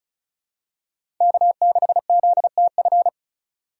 Morse code, heard as a keyed tone, about 34 wpm, 700 Hz. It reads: K6ZTF
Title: USA Call Signs - 5 Characters - 6x Character Spacing - 35wpm